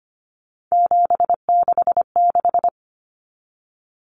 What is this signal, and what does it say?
Morse code, heard as a keyed tone, about 25 wpm, 700 Hz. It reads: 766
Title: Common Cut Numbers - 3-Digit 25wpm